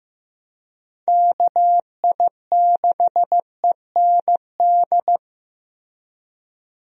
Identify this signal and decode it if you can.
Morse code, heard as a keyed tone, about 15 wpm, 700 Hz. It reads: KI6END
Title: USA Call Signs - 6 Characters - Repeated 3x - 15wpm